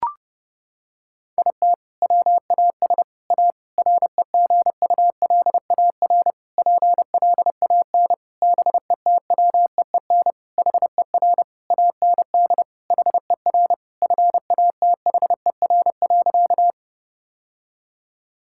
Morse code, heard as a keyed tone, about 32 wpm, 700 Hz. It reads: IT WAS A REGULAR PLAN BETWEEN HER AND HER FATHER.
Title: Sentences from Top 1000 Words - Review Part 6 - 30wpm